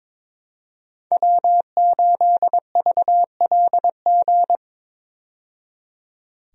Morse code, heard as a keyed tone, about 21 wpm, 700 Hz. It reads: W8VLG
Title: USA Call Signs - 5 Characters - 22wpm